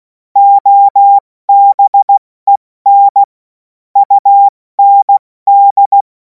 Morse code, heard as a keyed tone, about 16 wpm, 800 Hz. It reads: OBEN UND
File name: Morsekurs / Witze / Witz 03. März 2026 70 F80